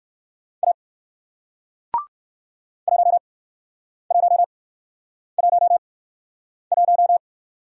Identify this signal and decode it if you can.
Morse code, heard as a keyed tone, about 60 wpm, spaced as for 9 wpm, 700 Hz. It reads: A 1111